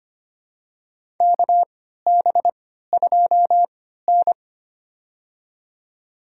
Morse code, heard as a keyed tone, about 25 wpm, spaced as for 14 wpm, 700 Hz. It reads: KB2N
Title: USA Call Signs - 4 Characters - 3x Character Spacing - 25wpm